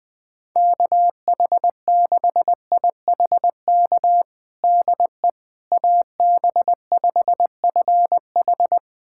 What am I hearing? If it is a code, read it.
Morse code, heard as a keyed tone, about 21 wpm, 700 Hz. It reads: KH6IHK DE AB5FH
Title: USA Callsign DE Callsign 20wpm